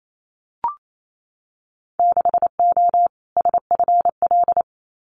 Morse code, heard as a keyed tone, about 28 wpm, 700 Hz. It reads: 6O SFL
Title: Winter Field Day Exchanges Encore 28wpm